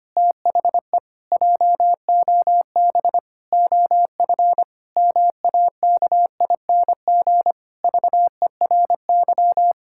Morse code, heard as a keyed tone, about 25 wpm, 700 Hz. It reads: THE JOB OF MAKING VERY
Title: Pangram Challenge v2 - 25wpm